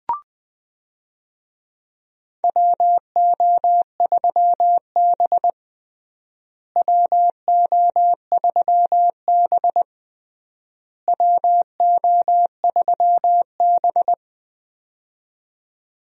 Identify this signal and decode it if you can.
Morse code, heard as a keyed tone, about 21 wpm, 700 Hz. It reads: WO3B WO3B WO3B
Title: USA Call Signs - 4 Characters - Repeated 3x - 20wpm